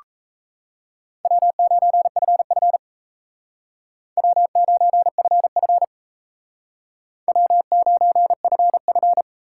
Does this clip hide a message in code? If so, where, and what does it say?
Yes: Morse code from 1.3 to 9.2 s, heard as a keyed tone, about 37 wpm, 700 Hz. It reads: W9FF W9FF W9FF